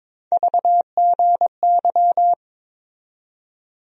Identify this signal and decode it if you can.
Morse code, heard as a keyed tone, about 22 wpm, 700 Hz. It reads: VGY
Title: USA Call Sign Postfixes - 3 Letters 22wpm